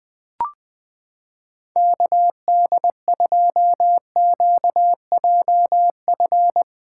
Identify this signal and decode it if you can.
Morse code, heard as a keyed tone, about 20 wpm, 700 Hz. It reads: KD2QJF
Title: USA Callsign DE Callsign - 6 Characters - 20wpm